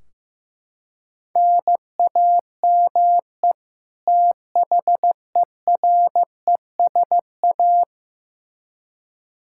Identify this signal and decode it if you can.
Morse code, heard as a keyed tone, about 15 wpm, 700 Hz. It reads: NAME THERESA
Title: K1USN SST QSO Practice 15wpm